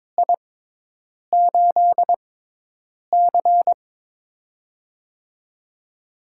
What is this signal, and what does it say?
Morse code, heard as a keyed tone, about 22 wpm, spaced as for 8 wpm, 700 Hz. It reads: I8C
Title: USA Call Signs - 4 Characters - 6x Character Spacing - 22wpm